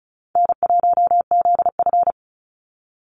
Morse code, heard as a keyed tone, about 34 wpm, 700 Hz. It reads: N1ZF
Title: NA SOTA Chaser Callsigns 35wpm